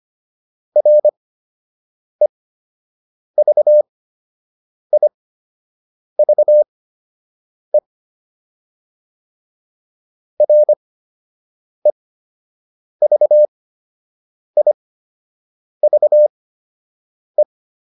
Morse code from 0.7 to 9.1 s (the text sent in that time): REVIVE